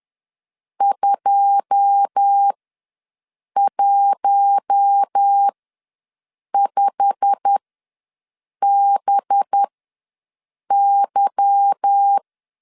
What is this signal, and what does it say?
Morse code, heard as a keyed tone, about 10 wpm, spaced as for 6 wpm, 800 Hz. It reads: 215BY
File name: LEZIONI / 9 2 ws 10wpm 4130